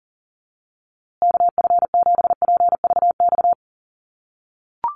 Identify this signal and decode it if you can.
Morse code, heard as a keyed tone, about 40 wpm, 700 Hz. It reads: KF7PVX
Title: USA Call Signs Repeated 3x 40wpm